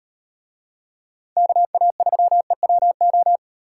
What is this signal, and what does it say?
Morse code, heard as a keyed tone, about 36 wpm, 700 Hz. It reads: KA3EWO